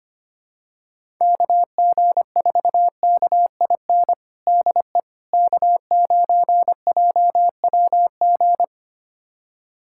Morse code, heard as a keyed tone, about 25 wpm, 700 Hz. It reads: KG4KIN DE K9JWG
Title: USA Callsign DE Callsign 25wpm